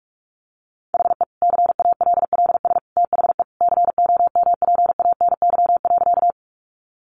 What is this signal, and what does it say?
Morse code, heard as a keyed tone, about 44 wpm, 700 Hz. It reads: HE CALLS THE COMPANY.
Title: Additional Easy Phrases 45wpm